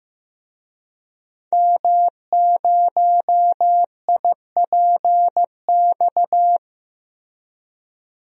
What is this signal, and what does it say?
Morse code, heard as a keyed tone, about 16 wpm, 700 Hz. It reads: M0IPX
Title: DXCC-100 Call Signs - 5 Characters - Repeated 2x - 15wpm